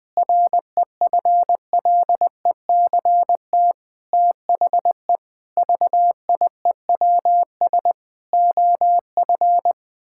Morse code, heard as a keyed tone, about 20 wpm, 700 Hz. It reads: REFLECT THE VIEWS OF
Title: Sentences with 8-Word Phrases - 20wpm